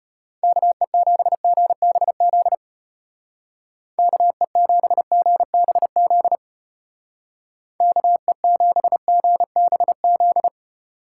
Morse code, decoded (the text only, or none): KE7GBZ KE7GBZ KE7GBZ